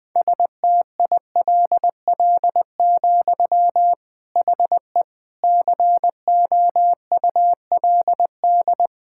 Morse code, heard as a keyed tone, about 20 wpm, 700 Hz. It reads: STILL, HE COULD
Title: Sentences from Top 1000 Words - Review Part 5 - 20wpm